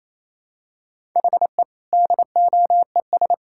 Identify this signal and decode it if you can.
Morse code, heard as a keyed tone, about 28 wpm, 700 Hz. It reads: HE DOES
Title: Sentences from Top 1000 Words - Review Part 6 - 28wpm